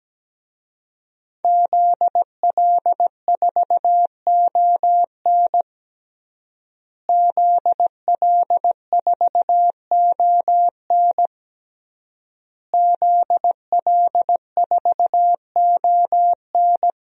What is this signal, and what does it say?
Morse code, heard as a keyed tone, about 17 wpm, 700 Hz. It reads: ZL4ON ZL4ON ZL4ON
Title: DXCC-100 Call Signs - 5 Characters - Repeated 3x - 17wpm